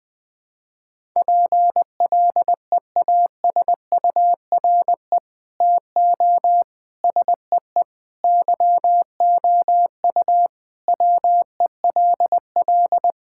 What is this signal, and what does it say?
Morse code, heard as a keyed tone, about 20 wpm, 700 Hz. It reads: PLEASURE TO SEE YOU WELL
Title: Sentences from Top 1600 Words 20wpm